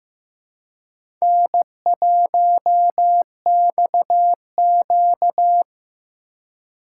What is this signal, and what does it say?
Morse code, heard as a keyed tone, about 15 wpm, 700 Hz. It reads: N1XQ
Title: USA Call Signs - 4 Characters - 3x Character Spacing - 15wpm